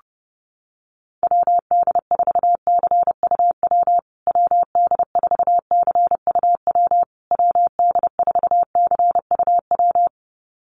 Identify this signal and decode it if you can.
Morse code, heard as a keyed tone, about 28 wpm, 700 Hz. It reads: WD4CUW WD4CUW WD4CUW
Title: USA Call Signs Repeated 3x 30wpm